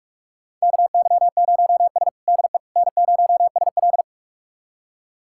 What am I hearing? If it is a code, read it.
Morse code, heard as a keyed tone, about 46 wpm, 700 Hz. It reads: KY0S DE N0SL
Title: USA Callsign DE Callsign - 4 Characters - 45wpm